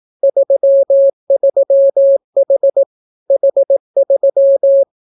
Morse code, heard as a keyed tone, about 18 wpm, 550 Hz. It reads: 33H H3